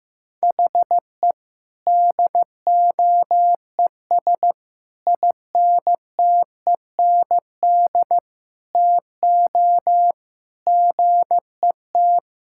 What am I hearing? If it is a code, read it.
Morse code, heard as a keyed tone, about 16 wpm, 700 Hz. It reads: HE DOES INTEND TO GET
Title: Sentences from Top 1600 Words 15wpm